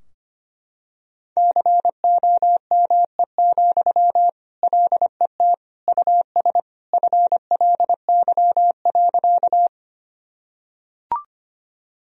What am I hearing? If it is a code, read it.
Morse code, heard as a keyed tone, about 25 wpm, 700 Hz. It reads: COME, LET US FLY.
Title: Sentences from Top 1000 Words - Review Part 2 - 25wpm